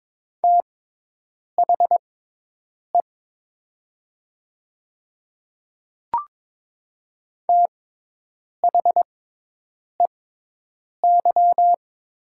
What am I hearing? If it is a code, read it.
Morse code, heard as a keyed tone, about 22 wpm, spaced as for 8 wpm, 700 Hz. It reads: THE THEY
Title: 6x Character Spacing - Top 100 Words - 22wpm